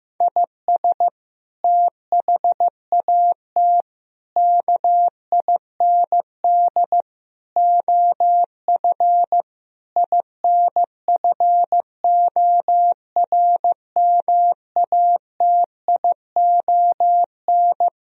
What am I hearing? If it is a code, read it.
Morse code, heard as a keyed tone, about 15 wpm, 700 Hz. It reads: IS THAT KIND OF INFORMATION
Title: Sentences from Top 1000 Words - Review Part 3 - 15wpm